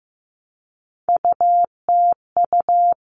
Morse code, heard as a keyed tone, about 15 wpm, 700 Hz. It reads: UTU
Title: All Cut Numbers - 3-Digit 15wpm